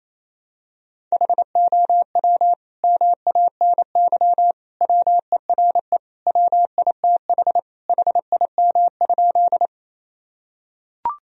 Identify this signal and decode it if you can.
Morse code, heard as a keyed tone, about 28 wpm, 700 Hz. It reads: HOW MANY WERE WITH HIM?
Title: Sentences from Top 1000 Words - Review Part 1 - 28wpm